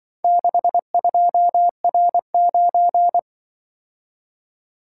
Morse code, heard as a keyed tone, about 24 wpm, 700 Hz. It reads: 62R9
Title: Speed-Racing - Four Letter-Number Combinations - 20wpm